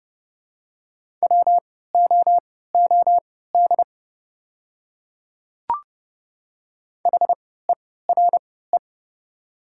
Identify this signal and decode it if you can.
Morse code, heard as a keyed tone, about 30 wpm, spaced as for 17 wpm, 690 Hz. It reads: WOOD HERE
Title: 3x Character Spacing - Top 500 Words - 30wpm